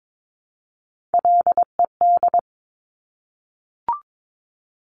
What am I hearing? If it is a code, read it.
Morse code, heard as a keyed tone, about 22 wpm, 700 Hz. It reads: LED
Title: Top Three Letter Combinations in Words 22wpm